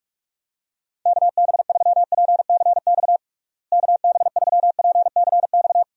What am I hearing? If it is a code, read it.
Morse code, heard as a keyed tone, about 46 wpm, 700 Hz. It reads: KB3PCX KB3PCX
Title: USA Call Signs - 6 Characters - Repeated 3x - 45wpm